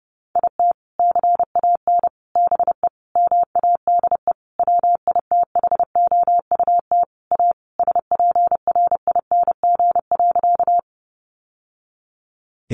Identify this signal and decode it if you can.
Morse code, heard as a keyed tone, about 30 wpm, 700 Hz. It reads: IT CAN BE MADE WITHOUT A SPRING.
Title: Sentences from Top 1200 Words 30wpm